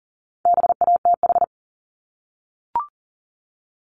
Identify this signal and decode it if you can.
Morse code, heard as a keyed tone, about 41 wpm, 700 Hz. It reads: BATH
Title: Multi-letter Phonemes 40wpm